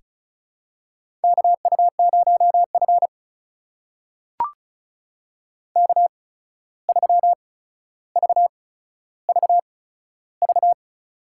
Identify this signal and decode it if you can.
Morse code, heard as a keyed tone, about 36 wpm, spaced as for 14 wpm, 700 Hz. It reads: KU0F K3VVV
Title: USA Call Signs - 8x Character Spacing - 35wpm